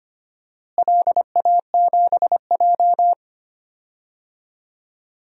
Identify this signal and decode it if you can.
Morse code, heard as a keyed tone, about 25 wpm, 700 Hz. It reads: LA7J
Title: DXCC-100 Call Signs - 4 Characters - Repeated 3x - 25wpm